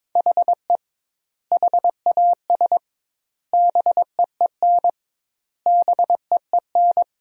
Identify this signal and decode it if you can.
Morse code, heard as a keyed tone, about 22 wpm, 700 Hz. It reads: HE HAS BEEN BEEN
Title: 2x Word-Spacing - Sentences from Top 300 Words - 22wpm